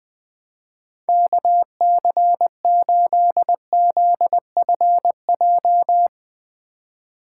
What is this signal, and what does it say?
Morse code, heard as a keyed tone, about 21 wpm, 700 Hz. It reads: KC8ZFJ
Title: USA Call Signs - 2x Character Spacing - 20wpm